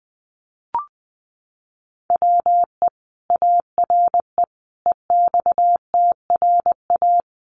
Morse code, heard as a keyed tone, about 20 wpm, 700 Hz. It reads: WE ARE EXTRA
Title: Sentences from Top 1000 Words 20wpm